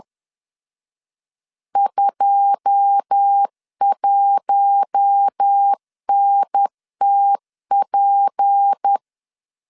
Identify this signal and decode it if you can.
Morse code, heard as a keyed tone, about 10 wpm, 800 Hz. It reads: 21NTP
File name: LEZIONI / 9 2867 10wpm 2733